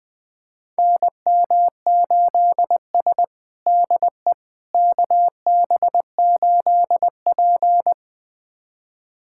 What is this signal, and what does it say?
Morse code, heard as a keyed tone, about 20 wpm, 700 Hz. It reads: NM8S DE KB8P
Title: USA Callsign DE Callsign - 4 Characters - 20wpm